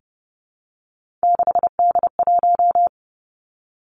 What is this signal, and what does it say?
Morse code, heard as a keyed tone, about 30 wpm, 700 Hz. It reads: 6D1
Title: Three Challenging Letter-Number Combinations 30wpm